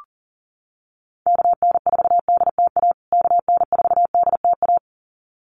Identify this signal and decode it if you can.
Morse code, heard as a keyed tone, about 37 wpm, 700 Hz. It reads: KN4DTA KN4DTA
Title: USA Call Signs Repeated 2x 40wpm